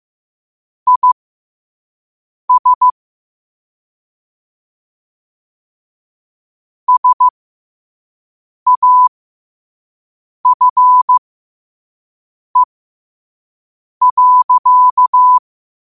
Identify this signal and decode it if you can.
Morse code, heard as a keyed tone, about 15 wpm, spaced as for 5 wpm, 1000 Hz. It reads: IS SAFE.
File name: qotdfiles / 5wpm